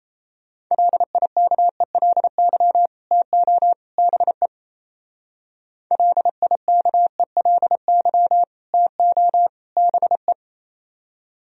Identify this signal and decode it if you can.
Morse code, heard as a keyed tone, about 33 wpm, 700 Hz. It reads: LIKELY TO BE LIKELY TO BE
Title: Speed-Racing - Sets of 3 Words 28wpm